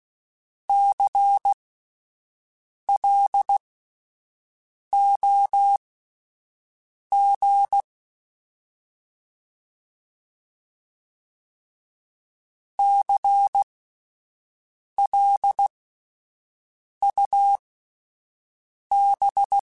Morse code, heard as a keyed tone, about 16 wpm, spaced as for 5 wpm, 800 Hz. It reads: CLOG CLUB
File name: Alphabetical / Alphabetical mp3s / Cs @5